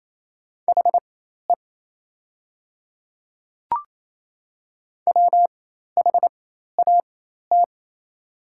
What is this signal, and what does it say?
Morse code, heard as a keyed tone, about 28 wpm, spaced as for 13 wpm, 700 Hz. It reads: HE WHAT
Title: 4x Character Spacing - Top 200 Words - 28wpm